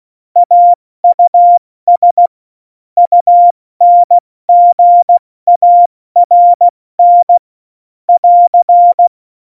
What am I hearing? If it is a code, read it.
Morse code, heard as a keyed tone, about 16 wpm, spaced as for 14 wpm, 700 Hz. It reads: AUS UNGARN <AR>
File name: Morsekurs / Witze / Witz 07. Mai 2019, 70 F80